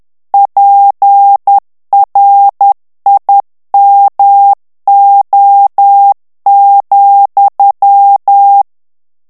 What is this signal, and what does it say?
Morse code, heard as a keyed tone, about 10 wpm, 800 Hz. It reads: PRIMO,
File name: ITALIEN texte clair 2 vitesse 10 mots minute